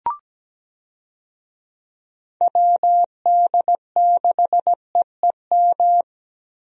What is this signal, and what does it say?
Morse code, heard as a keyed tone, about 17 wpm, 700 Hz. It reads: WD6EEM
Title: USA Call Signs - 6 Characters - Repeated 2x - 17wpm